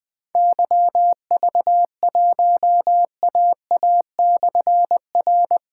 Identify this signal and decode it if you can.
Morse code, heard as a keyed tone, about 20 wpm, 700 Hz. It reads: YV1AA/R
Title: Difficult Call Signs 20wpm